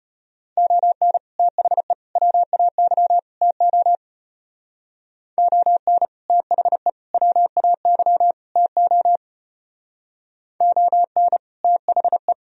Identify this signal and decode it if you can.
Morse code, heard as a keyed tone, about 30 wpm, 700 Hz. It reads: ON THE WAY TO ON THE WAY TO ON THE
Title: Speed-Racing - Sets of 4 Words 25wpm